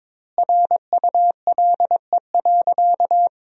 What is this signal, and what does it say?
Morse code, heard as a keyed tone, about 22 wpm, 700 Hz. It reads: RULE.
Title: 2x Word-Spacing - Sentences from Top 400 Words - 22wpm